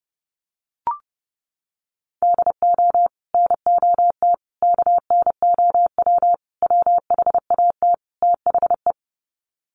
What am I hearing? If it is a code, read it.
Morse code, heard as a keyed tone, about 30 wpm, 700 Hz. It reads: DO NOT KNOW WHAT THE
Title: Sets of 5 Words 30wpm